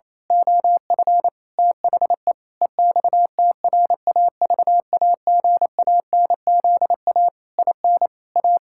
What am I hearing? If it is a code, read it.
Morse code, heard as a keyed tone, about 28 wpm, 700 Hz. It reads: OF THE EXTRAVAGANZA IN A
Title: Pangram Challenge v2 - 28wpm